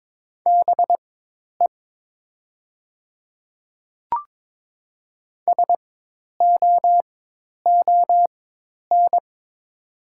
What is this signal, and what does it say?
Morse code, heard as a keyed tone, about 23 wpm, spaced as for 10 wpm, 700 Hz. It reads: BE SOON
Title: 4x Character Spacing - Top 500 Words - 22wpm